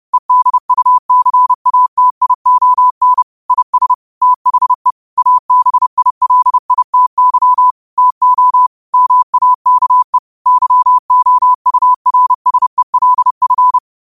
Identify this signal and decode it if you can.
Morse code, heard as a keyed tone, about 30 wpm, 1000 Hz. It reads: EDUCATION IS THE ABILITY TO MAKE YOURSELF
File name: qotdfiles / 30wpm